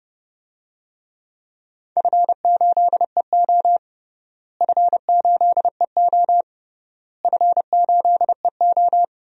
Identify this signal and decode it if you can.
Morse code, heard as a keyed tone, about 30 wpm, 700 Hz. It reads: F8EO F8EO F8EO
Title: DXCC-100 Call Signs - 4 Characters - Repeated 3x - 30wpm